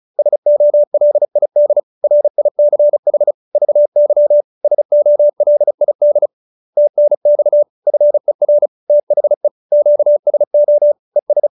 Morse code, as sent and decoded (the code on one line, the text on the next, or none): ... --- .-.. .. -.. / .-. .. -.-. .... / ...- -.-- / ... --- .-.. .. -.. / - -. -..- / ..-. .-. / - .... / --.- ... --- / . ...
SOLID RICH VY SOLID TNX FER THE QSO ES